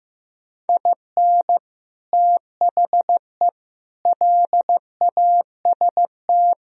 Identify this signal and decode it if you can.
Morse code, heard as a keyed tone, about 15 wpm, 700 Hz. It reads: IN THE LAST
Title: Speed-Racing - Sets of 5 Words 15wpm